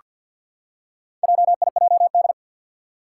Morse code, heard as a keyed tone, about 47 wpm, 700 Hz. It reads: JIJD
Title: Four Challenging Letter-Number Combinations 50wpm